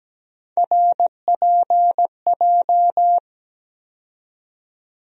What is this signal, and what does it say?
Morse code, heard as a keyed tone, about 17 wpm, 700 Hz. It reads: RPJ
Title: USA Call Sign Postfixes - 3 Letters 17wpm